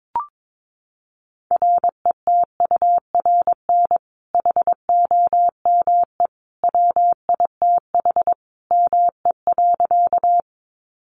Morse code, heard as a keyed tone, about 22 wpm, 700 Hz. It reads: RETURN HOME WITH ME.
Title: Sentences from Top 800 Words 22wpm